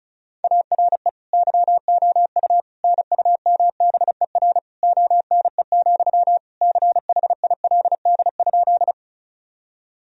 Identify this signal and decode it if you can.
Morse code, heard as a keyed tone, about 36 wpm, 700 Hz. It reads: ARE YOU NUMBER ONE, CHILD?
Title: Sentences from Top 1000 Words - Review Part 1 - 35wpm